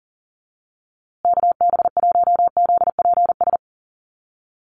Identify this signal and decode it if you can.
Morse code, heard as a keyed tone, about 40 wpm, 700 Hz. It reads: KB1ZPS
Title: USA Call Signs Repeated 2x 40wpm